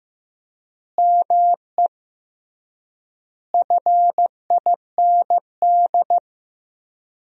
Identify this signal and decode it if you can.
Morse code, heard as a keyed tone, about 15 wpm, 700 Hz. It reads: ME FIND
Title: 3x Word-Spacing - Sentences from Top 200 Words - 15wpm